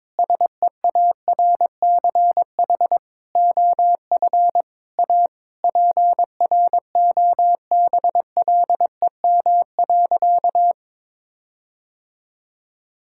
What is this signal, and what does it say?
Morse code, heard as a keyed tone, about 22 wpm, 700 Hz. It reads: SEARCH OF A PROBLEM.
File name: practice / 22wpm / Sentences with 8-Word Phrases - 22wpm